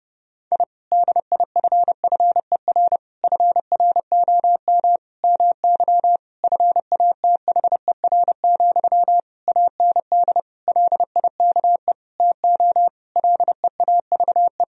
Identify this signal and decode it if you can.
Morse code, heard as a keyed tone, about 32 wpm, 700 Hz. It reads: I DIFFER FROM MY FATHER, AND LIKE TO LEAVE
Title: Sentences from Top 1000 Words - Review Part 1 - 30wpm